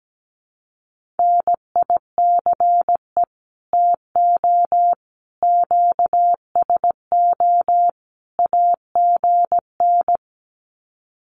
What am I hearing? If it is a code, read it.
Morse code, heard as a keyed tone, about 17 wpm, 700 Hz. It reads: NICE TO QSO AGN
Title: Common QSO Phrases 17wpm